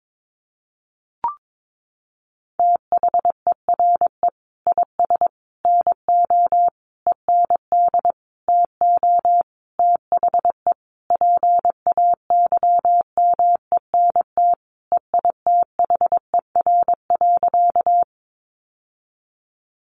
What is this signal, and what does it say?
Morse code, heard as a keyed tone, about 22 wpm, 700 Hz. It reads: THERE IS NO END TO THE PAYMENT EITHER.
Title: Sentences from Top 1200 Words 22wpm